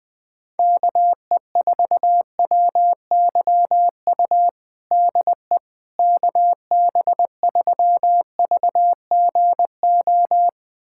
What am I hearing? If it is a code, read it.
Morse code, heard as a keyed tone, about 20 wpm, 700 Hz. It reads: KE4WYU DE KB3VGO
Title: USA Callsign DE Callsign - 6 Characters - 20wpm